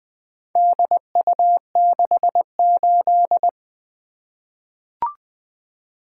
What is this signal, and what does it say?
Morse code, heard as a keyed tone, about 20 wpm, 700 Hz. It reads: DU68
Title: Four Letter-Number Combinations - 20wpm